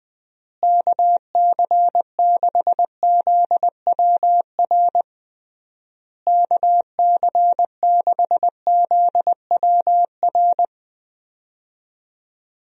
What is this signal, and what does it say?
Morse code, heard as a keyed tone, about 20 wpm, 700 Hz. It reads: KC6ZWR KC6ZWR
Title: USA Call Signs - 6 Characters - Repeated 3x - 20wpm